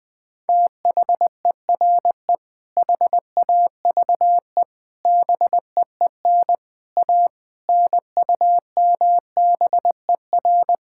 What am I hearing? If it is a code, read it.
Morse code, heard as a keyed tone, about 20 wpm, 700 Hz. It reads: THERE HAVE BEEN A NUMBER
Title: Sets of 6 Words - 20wpm